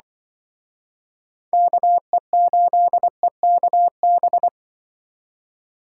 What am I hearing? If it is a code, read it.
Morse code, heard as a keyed tone, about 24 wpm, 700 Hz. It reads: KE8EKB
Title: Speed Racing - USA 2x3 Call Signs 20wpm